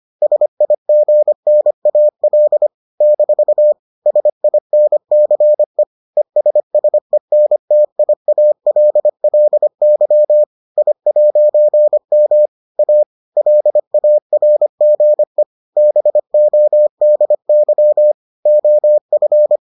Morse code, heard as a keyed tone, about 25 wpm, 600 Hz. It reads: SIGNAL - SINCE ESSENTIALLY I'M A LARGE BODY OF